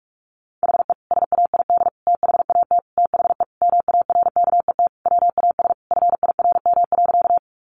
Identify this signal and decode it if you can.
Morse code, heard as a keyed tone, about 44 wpm, 700 Hz. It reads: HE SAID THAT THE MARKET WAS FIRM.
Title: Sentences from Top 1400 Words 45wpm